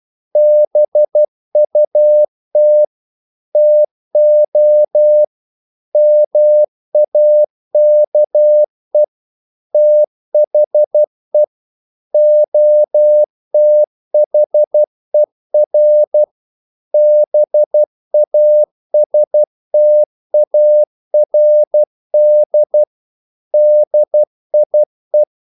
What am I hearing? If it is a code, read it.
Morse code, heard as a keyed tone, about 12 wpm, 600 Hz. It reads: BUT TO MAKE THE OTHER BASTARD DIE
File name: CW / Practice / EnglishQuotations0312